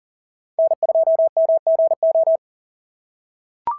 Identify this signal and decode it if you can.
Morse code, heard as a keyed tone, about 37 wpm, 650 Hz. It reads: N2MGO
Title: SOTA QSO Practice 40wpm